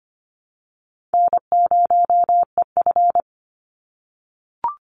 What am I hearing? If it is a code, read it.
Morse code, heard as a keyed tone, about 24 wpm, 700 Hz. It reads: N0EF
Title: Easy Call Signs - 25wpm